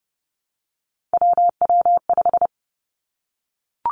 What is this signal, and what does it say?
Morse code, heard as a keyed tone, about 30 wpm, 700 Hz. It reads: WW5
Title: Three Challenging Letter-Number Combinations 30wpm